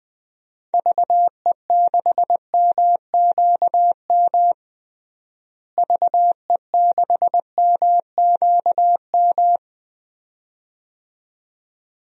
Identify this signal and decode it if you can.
Morse code, heard as a keyed tone, about 20 wpm, 700 Hz. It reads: VE6MQM VE6MQM
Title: DXCC-100 Call Signs - 6 Characters - Repeated 2x - 20wpm